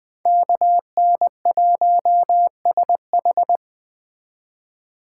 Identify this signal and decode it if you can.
Morse code, heard as a keyed tone, about 20 wpm, 700 Hz. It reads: KN1SH